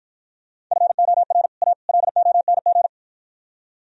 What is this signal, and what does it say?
Morse code, heard as a keyed tone, about 52 wpm, 700 Hz. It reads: FOR A LONG